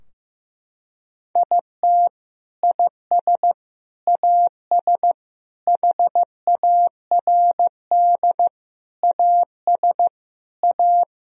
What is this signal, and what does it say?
Morse code, heard as a keyed tone, about 15 wpm, 700 Hz. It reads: IT IS AS HARD AS A
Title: Sentences from Top 1000 Words - Review Part 2 - 15wpm